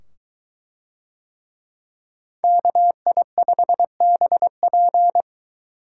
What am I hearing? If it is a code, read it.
Morse code, heard as a keyed tone, about 22 wpm, 700 Hz. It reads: KI5BP